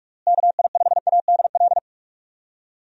Morse code, heard as a keyed tone, about 46 wpm, 700 Hz. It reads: KI5ADL
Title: USA Call Signs - 6 Characters - Repeated 3x - 45wpm